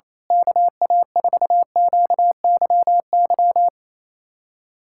Morse code, heard as a keyed tone, about 28 wpm, 700 Hz. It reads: KA4QYY
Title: USA Callsign DE Callsign 28wpm